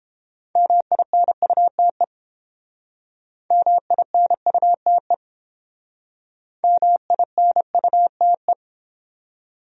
Morse code, heard as a keyed tone, about 30 wpm, 700 Hz. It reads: MINUTE MINUTE MINUTE